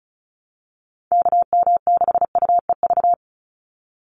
Morse code, heard as a keyed tone, about 34 wpm, 700 Hz. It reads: KM6UEV
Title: USA Call Signs 35wpm